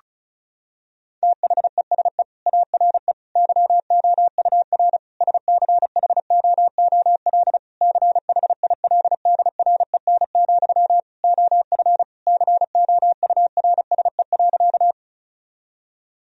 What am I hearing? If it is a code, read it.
Morse code, heard as a keyed tone, about 36 wpm, 700 Hz. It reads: THESE ARE YOUR SCHOOL CHILDREN, OF COURSE.
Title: Sentences from Top 1000 Words - Review Part 1 - 35wpm